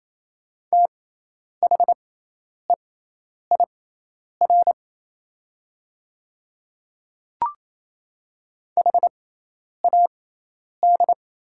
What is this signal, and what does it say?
Morse code, heard as a keyed tone, about 28 wpm, spaced as for 10 wpm, 700 Hz. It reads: THEIR HAD
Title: 6x Character Spacing - Top 200 Words - 28wpm